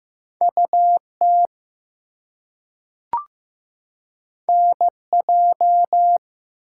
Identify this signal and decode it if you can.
Morse code, heard as a keyed tone, about 15 wpm, 700 Hz. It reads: UT NJ
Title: US States and Canadian Provinces 15wpm